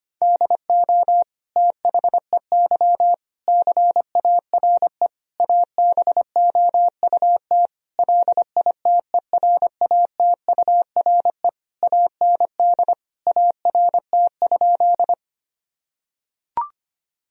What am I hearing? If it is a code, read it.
Morse code, heard as a keyed tone, about 25 wpm, 700 Hz. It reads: DO THEY CARE ABOUT LITERATURE AND ART?
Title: Sentences from Top 1000 Words - Review Part 3 - 25wpm